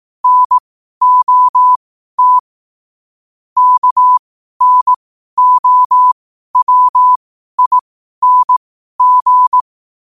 Morse code, heard as a keyed tone, about 18 wpm, spaced as for 12 wpm, 1000 Hz. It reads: NOT KNOWING
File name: qotdfiles / 12wpm